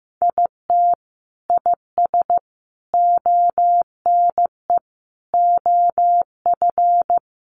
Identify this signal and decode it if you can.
Morse code, heard as a keyed tone, about 15 wpm, 700 Hz. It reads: IT IS ONE OF
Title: Sets of 5 Words 15wpm